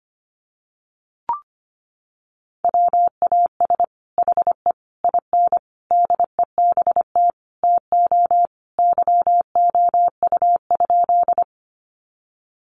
Morse code, heard as a keyed tone, about 25 wpm, 700 Hz. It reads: WAS HE IN DEBT TO YOU?
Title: Sentences from Top 1300 Words 25wpm